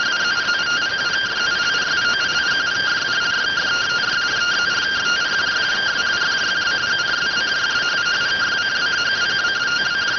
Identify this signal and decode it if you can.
Two-tone teleprinter signal, about 170 Hz shift.